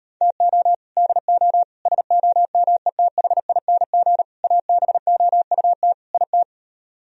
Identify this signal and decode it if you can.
Morse code, heard as a keyed tone, about 33 wpm, 700 Hz. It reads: TO DO SOMETHING ABOUT IT